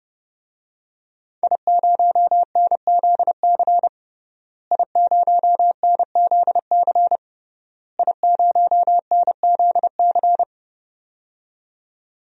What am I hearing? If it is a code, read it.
Morse code, heard as a keyed tone, about 30 wpm, 700 Hz. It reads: I0NZC I0NZC I0NZC
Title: DXCC-100 Call Signs - 5 Characters - Repeated 3x - 30wpm